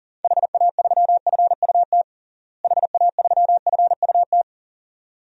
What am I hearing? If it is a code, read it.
Morse code, heard as a keyed tone, about 37 wpm, 700 Hz. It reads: HA3FUT HA3FUT
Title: DXCC-100 Call Signs - 6 Characters - Repeated 2x - 40wpm